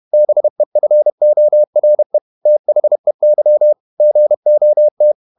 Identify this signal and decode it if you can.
Morse code, heard as a keyed tone, about 31 wpm, 600 Hz. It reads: BEFORE THEY GOT
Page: Never 31wpm | CW med Gnister